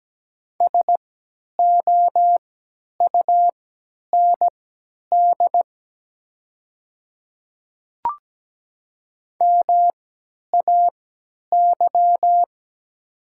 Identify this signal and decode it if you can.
Morse code, heard as a keyed tone, about 17 wpm, spaced as for 10 wpm, 700 Hz. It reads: SOUND MAY
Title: 3x Character Spacing - Top 100 Words - 17wpm